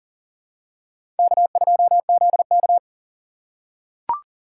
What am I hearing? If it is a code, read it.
Morse code, heard as a keyed tone, about 40 wpm, 700 Hz. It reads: K2ZK
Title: USA Call Signs - 4 Characters - 6x Character Spacing - 40wpm